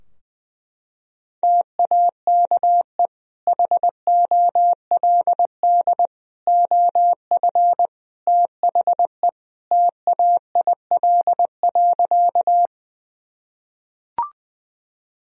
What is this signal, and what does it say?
Morse code, heard as a keyed tone, about 21 wpm, 700 Hz. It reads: TAKE HOLD OF THE TAIL.
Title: Sentences from Top 1000 Words - Review Part 3 - 20wpm